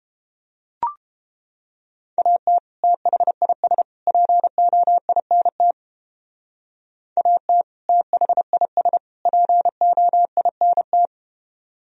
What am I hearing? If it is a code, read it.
Morse code, heard as a keyed tone, about 30 wpm, 700 Hz. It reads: AT THIS POINT AT THIS POINT